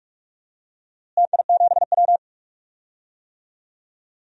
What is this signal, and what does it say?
Morse code, heard as a keyed tone, about 43 wpm, 700 Hz. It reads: TI7W
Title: DXCC-100 Call Signs - 4 Characters - Repeated 2x - 45wpm